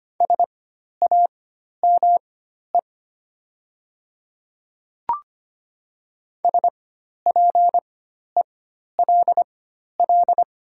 Morse code, heard as a keyed tone, about 25 wpm, spaced as for 12 wpm, 690 Hz. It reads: SAME SPELL
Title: 4x Character Spacing - Top 300 Words - 25wpm